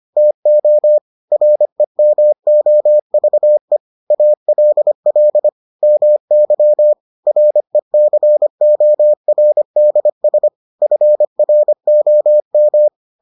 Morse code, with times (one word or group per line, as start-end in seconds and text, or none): 0.2-1.0 TO
1.3-3.8 REMOVE
4.1-5.5 ALL
5.8-6.9 MY
7.3-10.5 RECORDS
10.8-12.9 FROM